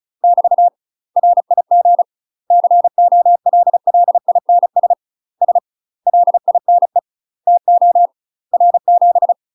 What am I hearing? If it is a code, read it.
Morse code, heard as a keyed tone, about 34 wpm, 700 Hz. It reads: <BT> RIG COLLINS S LINE TO R7
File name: UK MASTER QSOs TRAINING/35WPM